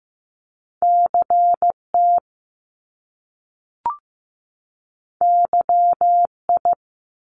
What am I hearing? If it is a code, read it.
Morse code, heard as a keyed tone, about 15 wpm, 700 Hz. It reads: CT YI
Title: Top Two Letter Combinations in Words 15wpm